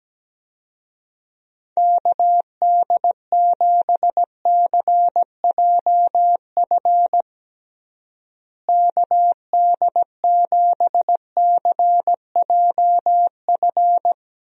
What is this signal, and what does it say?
Morse code, heard as a keyed tone, about 17 wpm, 700 Hz. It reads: KD7CJF KD7CJF
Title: USA Call Signs - 6 Characters - Repeated 3x - 17wpm